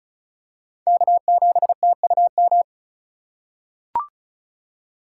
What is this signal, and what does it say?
Morse code, heard as a keyed tone, about 36 wpm, 700 Hz. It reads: K7TUM
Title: USA Call Signs - 3x Character Spacing - 35wpm